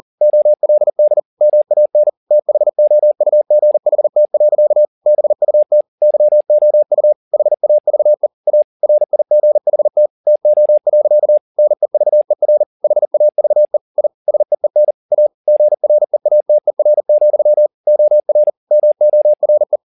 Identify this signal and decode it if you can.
Morse code, heard as a keyed tone, about 40 wpm, 600 Hz. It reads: OLD MAN THOUGHT. BUT YOU HAVE A RIGHT TO. NEVER HAVE I SEEN A GREATER, OR MORE
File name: the old man and the sea 2 40wpm